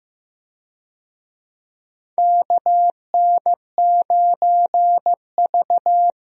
Morse code, heard as a keyed tone, about 15 wpm, 700 Hz. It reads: KN9V
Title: USA Call Signs - 4 Characters - 15wpm